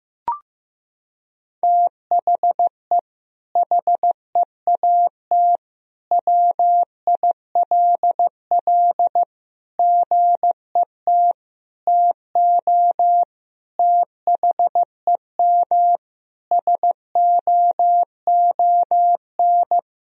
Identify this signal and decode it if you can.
Morse code, heard as a keyed tone, about 15 wpm, 700 Hz. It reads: THE HEAT WILL GET TO THEM SOON
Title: Sentences from Top 1000 Words - Review Part 3 - 15wpm